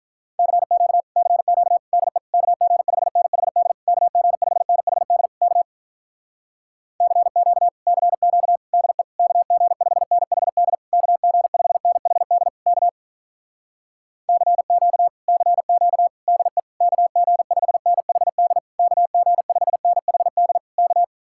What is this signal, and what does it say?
Morse code, heard as a keyed tone, about 47 wpm, 700 Hz. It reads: CQ CQ DE KG5NHD KG5NHD K CQ CQ DE KG5NHD KG5NHD K CQ CQ DE KG5NHD KG5NHD K